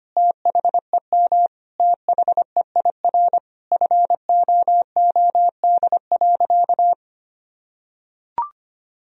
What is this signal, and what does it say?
Morse code, heard as a keyed tone, about 25 wpm, 700 Hz. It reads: THEM THEIR FOOD.
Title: Sentences from Top 1000 Words - Review Part 1 - 25wpm